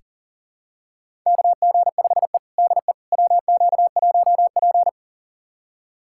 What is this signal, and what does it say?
Morse code, heard as a keyed tone, about 40 wpm, 700 Hz. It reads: KG5E DE WQ1P
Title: USA Callsign DE Callsign - 4 Characters - 40wpm